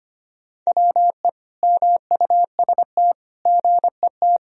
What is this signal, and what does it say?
Morse code, heard as a keyed tone, about 25 wpm, 700 Hz. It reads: WE MUST GET
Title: Sentences from Top 1000 Words - Review Part 2 - 25wpm